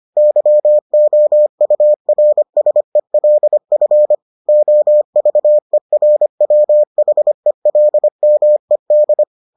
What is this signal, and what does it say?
Morse code, heard as a keyed tone, about 25 wpm, 600 Hz. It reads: YOURSELF OVERWHELMED